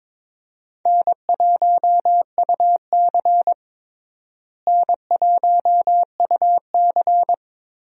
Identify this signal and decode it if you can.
Morse code, heard as a keyed tone, about 21 wpm, 700 Hz. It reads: N1UC N1UC
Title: USA Call Signs - 4 Characters - Repeated 2x - 22wpm